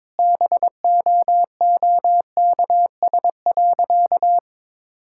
Morse code, heard as a keyed tone, about 22 wpm, 700 Hz. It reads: BOOKS.
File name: practice / 22wpm / Sentences with 4-Word Phrases - 22wpm